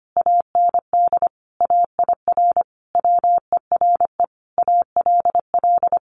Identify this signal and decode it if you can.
Morse code, heard as a keyed tone, about 25 wpm, 700 Hz. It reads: AND AIR WERE ALL
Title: Sentences from Top 1500 Words 25wpm